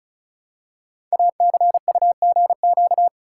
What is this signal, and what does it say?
Morse code, heard as a keyed tone, about 36 wpm, 700 Hz. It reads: ACUGQ
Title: Code Groups - 5 Letters - 35wpm